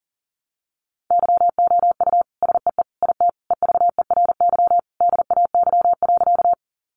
Morse code, heard as a keyed tone, about 40 wpm, 700 Hz. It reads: YOU SEE IT EVERY DAY.
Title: Sentences from Top 200 Words 40wpm